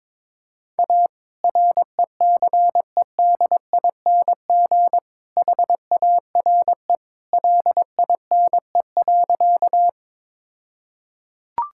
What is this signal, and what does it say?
Morse code, heard as a keyed tone, about 22 wpm, 690 Hz. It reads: A RECEDING HARE LINE.
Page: morse-code-ninja-practice-22wpm